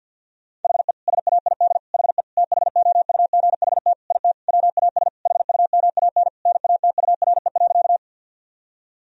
Morse code, heard as a keyed tone, about 52 wpm, 700 Hz. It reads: HE SAID HE THOUGHT IT WAS HUMAN NATURE.
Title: Sentences from Top 1000 Words - Review Part 3 - 50wpm